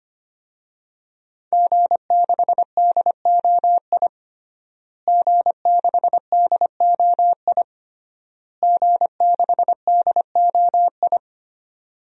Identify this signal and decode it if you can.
Morse code, heard as a keyed tone, about 25 wpm, 700 Hz. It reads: G6DOI G6DOI G6DOI
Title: DXCC-100 Call Signs - 5 Characters - Repeated 3x - 25wpm